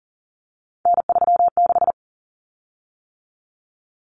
Morse code, heard as a keyed tone, about 40 wpm, 700 Hz. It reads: N36
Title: Common Cut Numbers - 3-Digit 40wpm